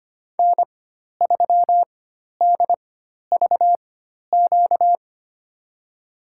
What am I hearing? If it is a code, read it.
Morse code, heard as a keyed tone, about 25 wpm, spaced as for 12 wpm, 700 Hz. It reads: N3DVQ
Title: USA Call Signs - 5 Characters - 4x Character Spacing - 25wpm